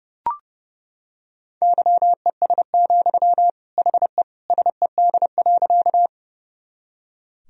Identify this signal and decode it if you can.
Morse code, heard as a keyed tone, about 28 wpm, 700 Hz. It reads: YES, HE SED.
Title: Sentences from Top 500 Words and Abbreviations 30wpm